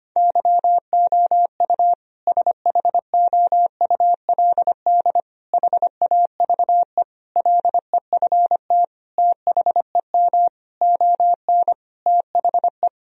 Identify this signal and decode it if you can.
Morse code, heard as a keyed tone, about 25 wpm, 700 Hz. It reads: YOU SHOULD HAVE LEFT THEM ON THE
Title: Sentences from Top 1000 Words - Review Part 1 - 25wpm